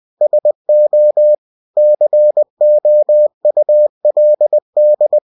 Morse code, heard as a keyed tone, about 20 wpm, 600 Hz. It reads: SO COULD
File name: cw-books / winnie-the-pooh / winnie-20wpm-page-0029